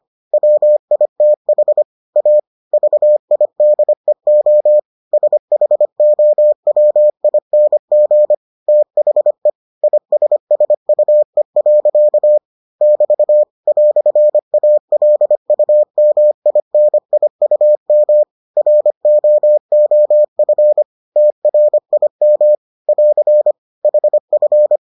Morse code, heard as a keyed tone, about 25 wpm, 600 Hz. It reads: WITH A VIDEO SHOWING THE ISSUE. <BT> "ALUMINIUM ROOF TRIM <AR> HF